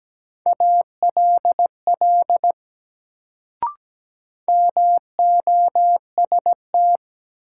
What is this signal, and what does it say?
Morse code, heard as a keyed tone, about 17 wpm, 700 Hz. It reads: ALL MOST
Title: Sets of 2 Words - New Word Drill - 17wpm